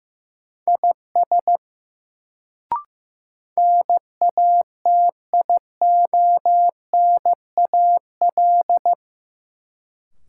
Morse code, heard as a keyed tone, about 15 wpm, 700 Hz. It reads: IS NATIONAL
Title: Sets of 2 Words - New Word Drill - 15wpm